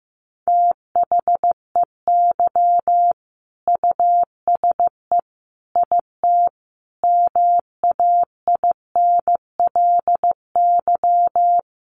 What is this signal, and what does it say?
Morse code, heard as a keyed tone, about 15 wpm, 700 Hz. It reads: THEY USE IT MAINLY
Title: Sentences from Top 1400 Words 15wpm